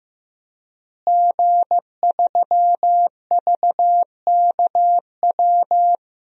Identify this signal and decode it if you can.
Morse code, heard as a keyed tone, about 15 wpm, 700 Hz. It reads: G3VKW
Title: DXCC-100 Call Signs - 5 Characters - Repeated 2x - 15wpm